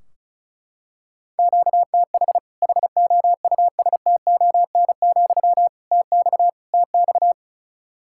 Morse code, heard as a keyed tone, about 34 wpm, 700 Hz. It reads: QTH HOUSTON, TX TX
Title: Practice QSO - Exchange 1 - Beginning - 35wpm